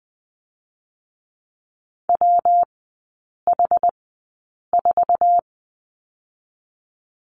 Morse code, heard as a keyed tone, about 20 wpm, spaced as for 8 wpm, 700 Hz. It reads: WH4
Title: 127 Lesson 21e - Number 6 - Three Characters 20wpm